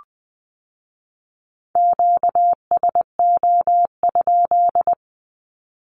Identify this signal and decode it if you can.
Morse code, heard as a keyed tone, about 20 wpm, 700 Hz. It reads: QSO?
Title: Q-Codes with Spoken Meaning - 20wpm